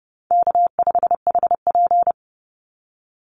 Morse code, heard as a keyed tone, about 30 wpm, 700 Hz. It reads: K5HP
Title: Call Signs Repeated 2x - 30wpm